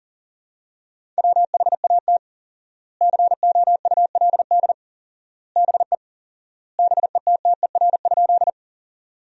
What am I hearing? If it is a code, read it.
Morse code, heard as a keyed tone, about 40 wpm, 700 Hz. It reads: WHAT COULD BE BETTER?
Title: 4x Word-Spacing - Sentences from Top 400 Words - 40wpm